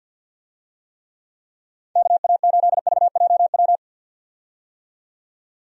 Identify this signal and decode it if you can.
Morse code, heard as a keyed tone, about 51 wpm, 700 Hz. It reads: KA8VJW
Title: USA Call Signs - 6 Characters - 50wpm